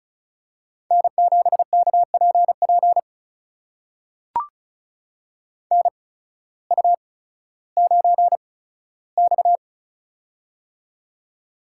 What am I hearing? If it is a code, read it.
Morse code, heard as a keyed tone, about 36 wpm, spaced as for 16 wpm, 700 Hz. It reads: N7KPP NU9X
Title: USA Call Signs - 8x Character Spacing - 35wpm